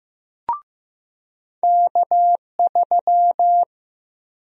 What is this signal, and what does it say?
Morse code, heard as a keyed tone, about 15 wpm, 700 Hz. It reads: K3
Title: USA Call Sign Prefixes - 15wpm